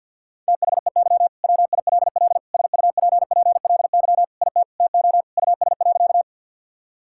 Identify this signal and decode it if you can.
Morse code, heard as a keyed tone, about 50 wpm, 700 Hz. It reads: THEY WILL SUPPLY IT TO US.
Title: Sentences from Top 1000 Words - Review Part 5 - 50wpm